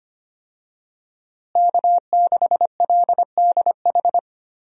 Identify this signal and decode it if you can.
Morse code, heard as a keyed tone, about 25 wpm, 700 Hz. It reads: K6LDH
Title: USA Call Signs - 5 Characters - Repeated 2x - 25wpm